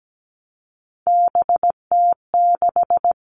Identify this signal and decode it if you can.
Morse code, heard as a keyed tone, about 17 wpm, 700 Hz. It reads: BT6
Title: All Cut Numbers - 3-Digit 17wpm